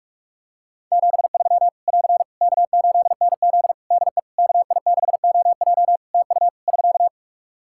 Morse code, heard as a keyed tone, about 46 wpm, 700 Hz. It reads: 73 <AR> K8NZ DE KI6OJ TU <SK>
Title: Practice QSO - Exchange 4 - Ending - 45wpm